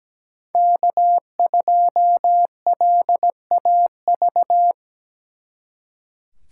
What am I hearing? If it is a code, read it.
Morse code, heard as a keyed tone, about 17 wpm, 700 Hz. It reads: K2LAV
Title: Speed Racing - USA 1x3 Call Signs 17wpm